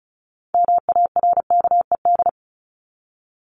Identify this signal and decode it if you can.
Morse code, heard as a keyed tone, about 34 wpm, 700 Hz. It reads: MARKED
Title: ED Suffix - Top 100 Words with Base Words 35wpm